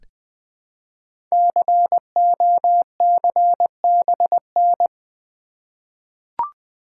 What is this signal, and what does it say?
Morse code, heard as a keyed tone, about 20 wpm, 700 Hz. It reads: COCBN
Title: Code Groups - 5 Letters - 20wpm